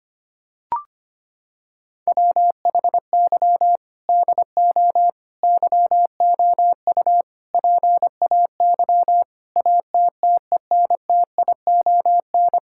Morse code, heard as a keyed tone, about 25 wpm, 700 Hz. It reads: WHY DO YOU PAY ATTENTION
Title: Sentences from Top 1000 Words - Review Part 6 - 25wpm